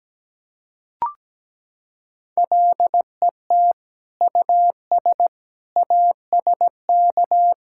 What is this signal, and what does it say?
Morse code, heard as a keyed tone, about 17 wpm, 700 Hz. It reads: LET US ASK
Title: Sentences from Top 1600 Words 17wpm